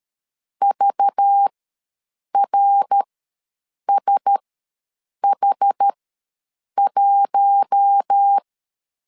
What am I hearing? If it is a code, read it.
Morse code, heard as a keyed tone, about 13 wpm, spaced as for 7 wpm, 800 Hz. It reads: VRSH1
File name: LEZIONI / 5 LBVC / 5 LBVC ws 12wpm 3159